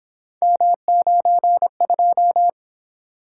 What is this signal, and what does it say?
Morse code, heard as a keyed tone, about 27 wpm, 700 Hz. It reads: M92
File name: Speed-Racing - Three Letter-Number Combinations - 22wpm